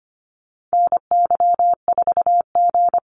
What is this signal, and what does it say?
Morse code, heard as a keyed tone, about 25 wpm, 700 Hz. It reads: NY4G
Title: NA SOTA Activator Callsigns 25wpm